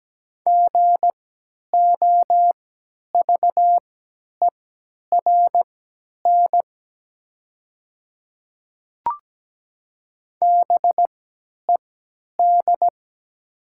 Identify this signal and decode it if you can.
Morse code, heard as a keyed tone, about 17 wpm, spaced as for 10 wpm, 700 Hz. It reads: GOVERN BED
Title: 3x Character Spacing - Top 500 Words - 17wpm